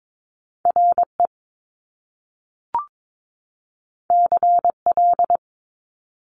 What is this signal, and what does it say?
Morse code, heard as a keyed tone, about 22 wpm, 700 Hz. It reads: RE CL
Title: Top Two Letter Combinations in Words 22wpm